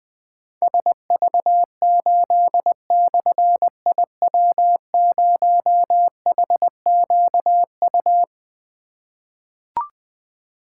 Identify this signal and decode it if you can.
Morse code, heard as a keyed tone, about 20 wpm, 700 Hz. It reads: SV8/IW0HQU
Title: Difficult Call Signs 20wpm